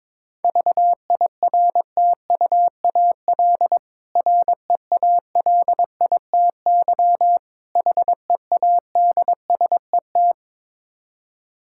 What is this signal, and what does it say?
Morse code, heard as a keyed tone, about 22 wpm, 690 Hz. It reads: VIRTUAL REALITY HEADSET
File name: Difficult Words with Hints - 22wpm